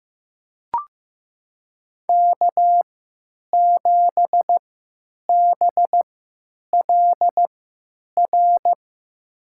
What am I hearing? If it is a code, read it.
Morse code, heard as a keyed tone, about 15 wpm, spaced as for 9 wpm, 700 Hz. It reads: K7BLR
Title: USA Call Signs - 5 Characters - 3x Character Spacing - 15wpm